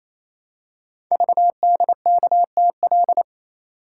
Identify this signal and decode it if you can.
Morse code, heard as a keyed tone, about 27 wpm, 700 Hz. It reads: VDKTL
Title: Code Groups - 5 Letters - 28wpm